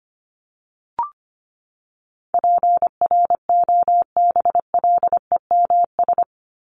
Morse code, heard as a keyed tone, about 25 wpm, 700 Hz. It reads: PROBLEMS
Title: ES:S Suffix - Top 100 Words with Base Words 25wpm